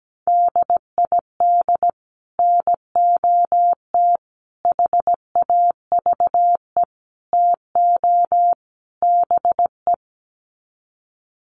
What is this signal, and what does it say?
Morse code, heard as a keyed tone, about 17 wpm, 700 Hz. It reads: DID NOT HAVE TO BE
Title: Sets of 5 Words Encore2 17wpm